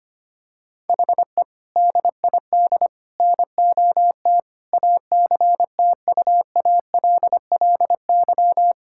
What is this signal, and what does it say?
Morse code, heard as a keyed tone, about 25 wpm, 700 Hz. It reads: HE DID NOT ACTUALLY
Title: Sentences from Top 1000 Words - Review Part 6 - 25wpm